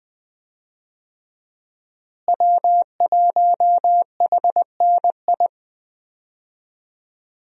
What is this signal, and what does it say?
Morse code, heard as a keyed tone, about 20 wpm, 700 Hz. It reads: W1HNI
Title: USA Call Signs - 5 Characters - Repeated 2x - 20wpm